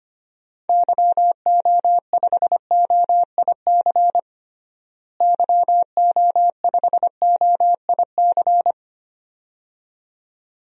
Morse code, heard as a keyed tone, about 25 wpm, 700 Hz. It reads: YO5OIC YO5OIC
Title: DXCC-100 Call Signs - 6 Characters - Repeated 2x - 25wpm